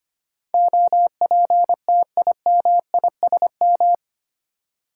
Morse code, heard as a keyed tone, about 25 wpm, 700 Hz. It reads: OPTIMISM
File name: Eight Letter Words - 25wpm